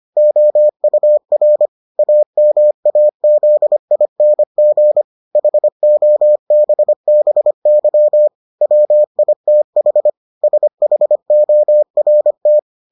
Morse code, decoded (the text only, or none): OUR AMAZING HOBBY WITH SHORT